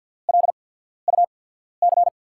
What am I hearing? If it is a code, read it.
Morse code, heard as a keyed tone, about 53 wpm, spaced as for 14 wpm, 700 Hz. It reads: LUC
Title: USA Call Signs - 5 Characters - 8x Character Spacing - 50wpm